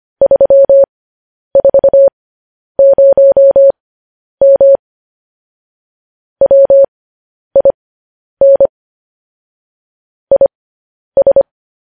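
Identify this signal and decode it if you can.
Morse code, heard as a keyed tone, about 25 wpm, spaced as for 10 wpm, 550 Hz. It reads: 340M WIN IS
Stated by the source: Morse code news headlines sent at 10 words per minute.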